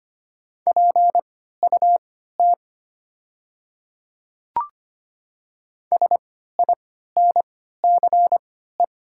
Morse code, heard as a keyed tone, about 25 wpm, spaced as for 14 wpm, 700 Hz. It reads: PUT SINCE
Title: 3x Character Spacing - Top 300 Words - 25wpm